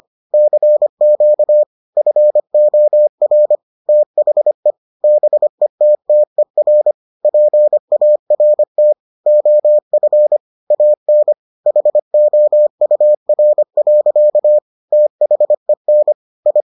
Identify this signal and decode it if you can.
Morse code, heard as a keyed tone, about 25 wpm, 600 Hz. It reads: CQ FOR THE BETTER PART OF AN HOUR. THEN I